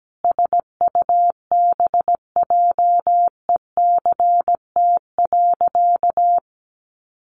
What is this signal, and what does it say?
Morse code, heard as a keyed tone, about 17 wpm, 700 Hz. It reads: SUBJECT.
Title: Sentences from Top 1400 Words 17wpm